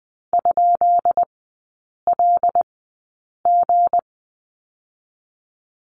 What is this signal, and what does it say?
Morse code, heard as a keyed tone, about 20 wpm, spaced as for 8 wpm, 690 Hz. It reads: ?LG
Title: 159 Lesson 26e - Letter g - Three Characters 20wpm